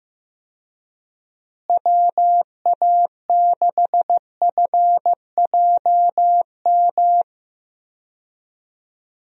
Morse code, heard as a keyed tone, about 15 wpm, 700 Hz. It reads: WA6FJM
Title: USA Call Signs - 6 Characters - Repeated 2x - 15wpm